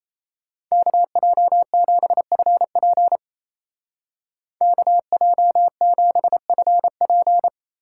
Morse code, heard as a keyed tone, about 34 wpm, 700 Hz. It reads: KJ7FP KJ7FP
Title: Speed Racing - USA 2x2 Call Signs 28wpm